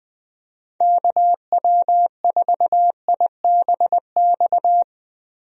Morse code, heard as a keyed tone, about 19 wpm, 700 Hz. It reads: KW4IBX
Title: USA Call Signs - 6 Characters - Repeated 3x - 20wpm